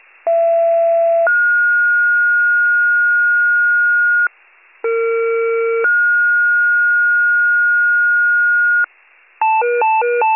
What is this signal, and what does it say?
Two-tone page, heard as a one-second tone, then a three-second tone, about 670 then 1430 Hz, about 460 then 1430 Hz. It is heard from 0.3 to 8.8 s.